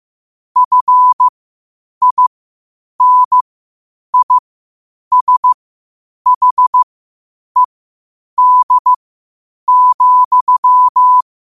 Morse code, heard as a keyed tone, about 15 wpm, spaced as for 8 wpm, 1000 Hz. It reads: FINISHED,
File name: qotdfiles / 8wpm